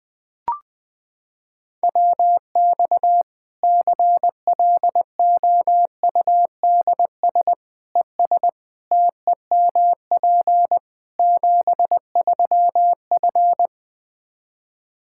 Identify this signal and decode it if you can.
Morse code, heard as a keyed tone, about 20 wpm, 700 Hz. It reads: WX CLOUDS ES TEMP 73F
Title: Practice QSO - Exchange 2 - Info - 20wpm